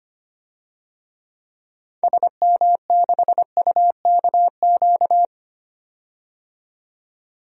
Morse code, heard as a keyed tone, about 25 wpm, 700 Hz. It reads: SM6UKQ
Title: DXCC-100 Call Signs - 6 Characters - 25wpm